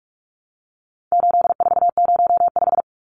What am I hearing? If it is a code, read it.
Morse code, heard as a keyed tone, about 44 wpm, 700 Hz. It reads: 8405
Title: Numbers - 4-Digit 45wpm